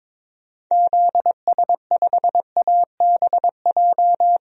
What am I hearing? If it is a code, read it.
Morse code, heard as a keyed tone, about 22 wpm, 700 Hz. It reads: ZS5ABJ
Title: DXCC-100 Call Signs - 6 Characters - 22wpm